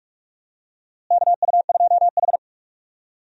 Morse code, heard as a keyed tone, about 46 wpm, 700 Hz. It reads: KU2H
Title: USA Call Signs - 4 Characters - Repeated 3x - 45wpm